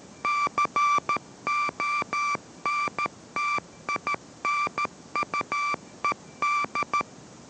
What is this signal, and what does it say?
Morse code, heard as a keyed tone, about 13 wpm, 1170 Hz. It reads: CONTINUED